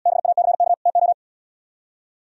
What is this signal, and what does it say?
Morse code, heard as a keyed tone, about 77 wpm, 700 Hz. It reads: BACK TO